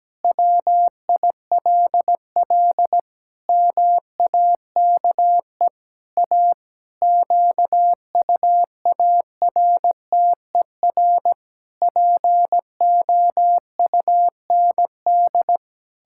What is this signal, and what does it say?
Morse code, heard as a keyed tone, about 17 wpm, 700 Hz. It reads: WILL MAKE A QUARTER POUND
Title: Pangram Challenge v2 - 17wpm